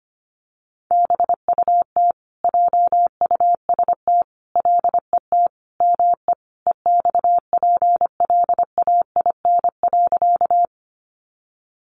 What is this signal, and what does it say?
Morse code, heard as a keyed tone, about 25 wpm, 700 Hz. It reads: BUT JUST LET ME EXPLAIN.
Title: Sentences from Top 1100 Words 25wpm